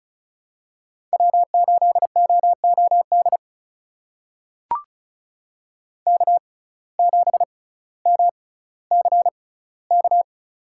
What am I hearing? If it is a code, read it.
Morse code, heard as a keyed tone, about 36 wpm, spaced as for 18 wpm, 700 Hz. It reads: W8OOD K7MCK
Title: USA Call Signs - 5 Characters - 6x Character Spacing - 35wpm